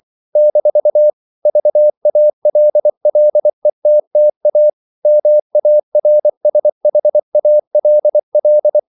Morse code, heard as a keyed tone, about 24 wpm, 600 Hz. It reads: - VALLETTA MARSHALL
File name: CW / Practice / NationalCapitals24